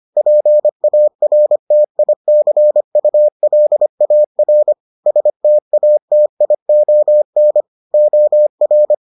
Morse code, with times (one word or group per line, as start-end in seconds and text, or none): 0.2-4.7 PARTICULAR
5.1-7.6 STATION
7.9-8.9 OR